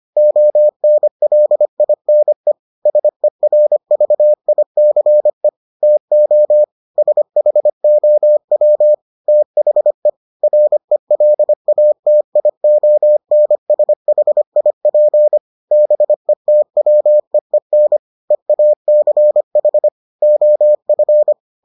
Morse code, heard as a keyed tone, about 25 wpm, 600 Hz. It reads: ONLINE SERVICE TO SHOW THE RELATIONSHIP BETWEEN EACH OF